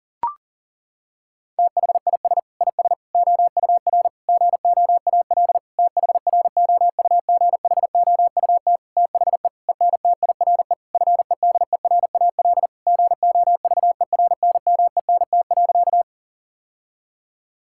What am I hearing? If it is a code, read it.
Morse code, heard as a keyed tone, about 40 wpm, 700 Hz. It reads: THIS IS OUR GOAL THROUGHOUT THE ENTIRE FEDERAL GOVERNMENT.
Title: Sentences from Top 1000 Words - Review Part 5 - 40wpm